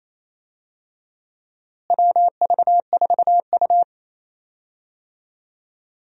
Morse code, heard as a keyed tone, about 27 wpm, 700 Hz. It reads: WV4U
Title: USA Call Signs - 4 Characters - Repeated 2x - 28wpm